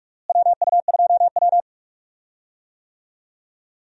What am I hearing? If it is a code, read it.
Morse code, heard as a keyed tone, about 45 wpm, 700 Hz. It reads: WU2W
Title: USA Call Signs - 4 Characters - Repeated 3x - 45wpm